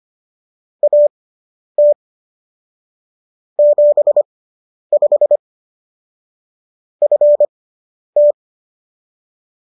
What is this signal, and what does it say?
Morse code, heard as a keyed tone, about 25 wpm, spaced as for 10 wpm, 590 Hz. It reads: AT 75 FT